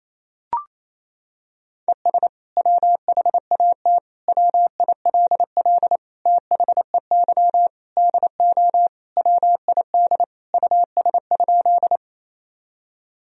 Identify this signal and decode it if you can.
Morse code, heard as a keyed tone, about 28 wpm, 700 Hz. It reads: ES WHAT WILL THEY DO WID US?
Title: Sentences from Top 200 Words and Abbreviations 28wpm